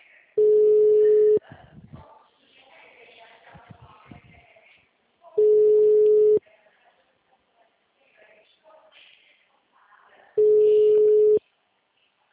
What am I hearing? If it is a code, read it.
Ringback tone (Europe, 425 Hz), with speech.
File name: out- -2008- - - .5641